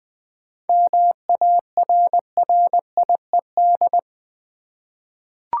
Morse code, heard as a keyed tone, about 20 wpm, 700 Hz. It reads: MARRIED
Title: Seven Letter Words - 20wpm